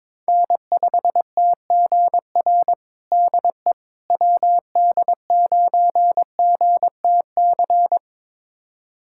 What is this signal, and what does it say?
Morse code, heard as a keyed tone, about 22 wpm, 700 Hz. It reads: N5TGR DE WD9GTC
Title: USA Callsign DE Callsign 22wpm